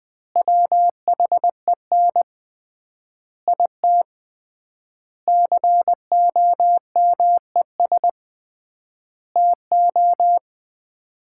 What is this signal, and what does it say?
Morse code, heard as a keyed tone, about 20 wpm, 700 Hz. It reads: WHEN IT COMES TO
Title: 3x Word-Spacing - Sets of 5 Words - 20wpm